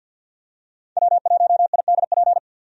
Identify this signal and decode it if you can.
Morse code, heard as a keyed tone, about 50 wpm, 700 Hz. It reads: W1IDP
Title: USA Call Signs - 5 Characters - 50wpm